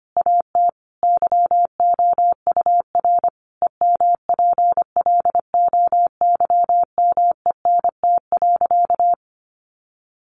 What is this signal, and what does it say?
Morse code, heard as a keyed tone, about 25 wpm, 700 Hz. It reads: AT YOUR EMPLOYMENT.
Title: Sentences from Top 1500 Words 25wpm